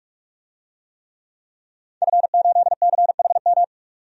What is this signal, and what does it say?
Morse code, heard as a keyed tone, about 46 wpm, 700 Hz. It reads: F8CHM
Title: DXCC-100 Call Signs - 5 Characters - 45wpm